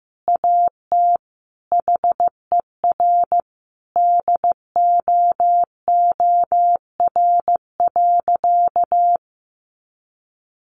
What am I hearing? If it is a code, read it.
Morse code, heard as a keyed tone, about 14 wpm, 700 Hz. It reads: AT HER DOOR.
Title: Sentences from Top 1400 Words 15wpm